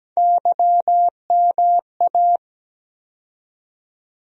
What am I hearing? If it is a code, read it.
Morse code, heard as a keyed tone, about 17 wpm, 700 Hz. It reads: YMA
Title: USA Call Sign Postfixes - 3 Letters 17wpm